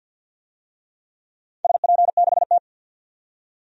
Morse code, heard as a keyed tone, about 45 wpm, 700 Hz. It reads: SP6T
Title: DXCC-100 Call Signs - 4 Characters - 50wpm